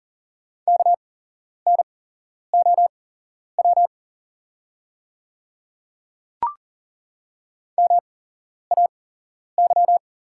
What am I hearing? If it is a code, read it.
Morse code, heard as a keyed tone, about 40 wpm, spaced as for 11 wpm, 690 Hz. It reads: KNOW MAY
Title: 8x Character Spacing - Top 100 Words - 40wpm